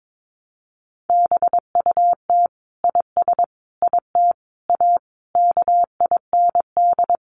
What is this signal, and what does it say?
Morse code, heard as a keyed tone, about 22 wpm, 700 Hz. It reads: BUT IS IT A KIND
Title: Sentences from Top 1200 Words 22wpm